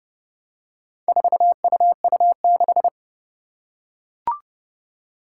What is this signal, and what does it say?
Morse code, heard as a keyed tone, about 30 wpm, 700 Hz. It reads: 4UU6
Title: Four Challenging Letter-Number Combinations 30wpm